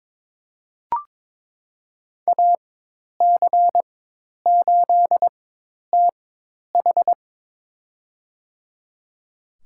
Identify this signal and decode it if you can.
Morse code, heard as a keyed tone, about 22 wpm, spaced as for 10 wpm, 700 Hz. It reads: AC8TH
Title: USA Call Signs - 4x Character Spacing - 22wpm